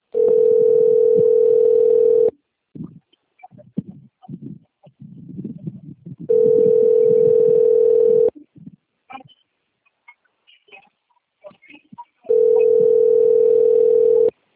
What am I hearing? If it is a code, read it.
Ringback tone (North America), 440 + 480 Hz.
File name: rg-1600-unknown- - - .11552